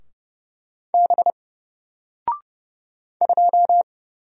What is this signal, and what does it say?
Morse code, heard as a keyed tone, about 30 wpm, 700 Hz. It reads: B 2
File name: Single-Letter-Number-30wpm